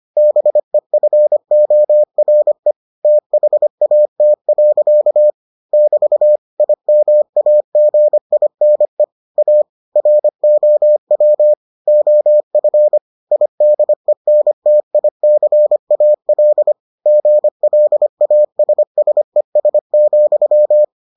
Morse code, heard as a keyed tone, about 25 wpm, 600 Hz. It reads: BEFORE THAT. <BT> IMAGINE A ROW OF IDENTICAL GLASSES,